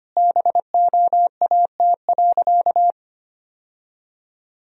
Morse code, heard as a keyed tone, about 25 wpm, 700 Hz. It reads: BOAT.
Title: Sentences from Top 1000 Words - Review Part 3 - 25wpm